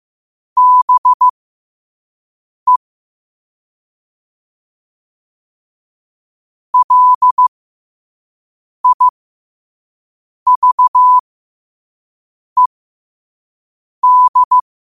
Morse code, heard as a keyed tone, about 15 wpm, spaced as for 5 wpm, 1000 Hz. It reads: BE LIVED
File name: qotdfiles / 5wpm